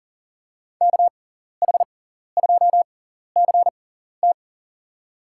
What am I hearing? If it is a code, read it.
Morse code, heard as a keyed tone, about 40 wpm, spaced as for 14 wpm, 700 Hz. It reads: KH2CT
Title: USA Call Signs - 6x Character Spacing - 40wpm